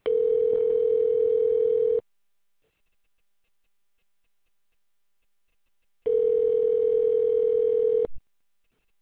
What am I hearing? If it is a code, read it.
Ringback tone (North America), 440 + 480 Hz.